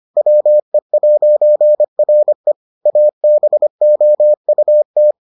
Morse code, heard as a keyed tone, about 25 wpm, 600 Hz. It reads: WE'RE ABOUT